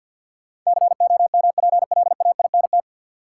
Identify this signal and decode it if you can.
Morse code, heard as a keyed tone, about 51 wpm, 700 Hz. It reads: COMPLAINT
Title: Sentences from Top 1400 Words - New Word Drill Alt - 50wpm